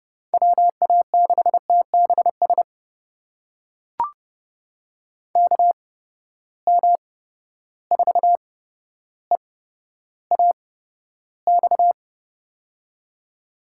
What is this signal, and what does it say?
Morse code, heard as a keyed tone, about 30 wpm, spaced as for 13 wpm, 700 Hz. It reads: WA6TBS KM4EAX
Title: USA Call Signs - 6 Characters - 8x Character Spacing - 30wpm